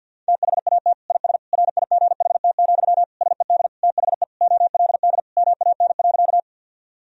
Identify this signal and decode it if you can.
Morse code, heard as a keyed tone, about 50 wpm, 700 Hz. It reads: THAT IS RIGHT, SED THE OLD MAN.
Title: Sentences from Top 200 Words and Abbreviations 50wpm